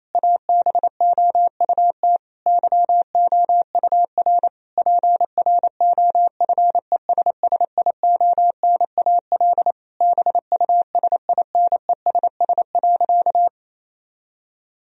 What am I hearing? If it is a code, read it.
Morse code, heard as a keyed tone, about 28 wpm, 700 Hz. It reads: ABOUT YOUR PROFESSIONAL BUSINESS.
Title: Sentences from Top 1000 Words - Review Part 4 - 28wpm